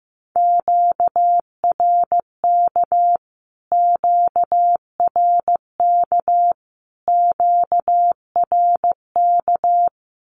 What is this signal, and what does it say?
Morse code, heard as a keyed tone, about 15 wpm, 700 Hz. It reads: QRK QRK QRK
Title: Q-Codes Repeated 3x 15wpm